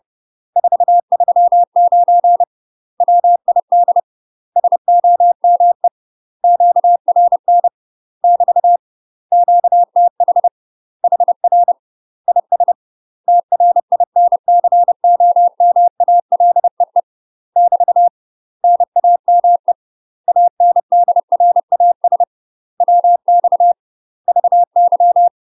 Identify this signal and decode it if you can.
Morse code, heard as a keyed tone, about 30 wpm, 700 Hz. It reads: 439 WID SOME QRN <BT> QTH HR IS TRINCOMALEE <BT> NAME ANDRAS WX VY
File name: UK MASTER QSOs TRAINING / 30WPM / UK MASTER QSOs 0035